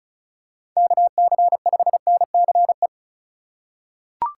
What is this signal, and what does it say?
Morse code, heard as a keyed tone, about 36 wpm, 700 Hz. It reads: KC5NCE
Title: USA Call Signs - 6 Characters - 4x Character Spacing - 35wpm